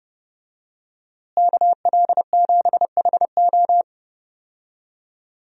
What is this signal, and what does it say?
Morse code, heard as a keyed tone, about 29 wpm, 700 Hz. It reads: KL7HO
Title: DXCC-100 Call Signs - 5 Characters - Repeated 2x - 30wpm